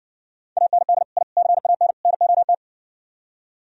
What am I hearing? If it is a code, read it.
Morse code, heard as a keyed tone, about 70 wpm, 700 Hz. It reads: AND I CAN NOT